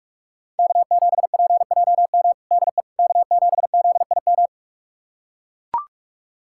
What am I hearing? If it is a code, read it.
Morse code, heard as a keyed tone, about 42 wpm, 700 Hz. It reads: K7PJM DE K7ZIM
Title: USA Callsign DE Callsign - 5 Characters - 45wpm